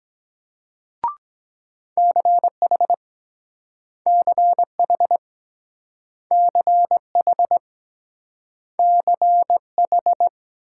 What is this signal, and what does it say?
Morse code, heard as a keyed tone, about 23 wpm, 690 Hz. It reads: CH CH CH CH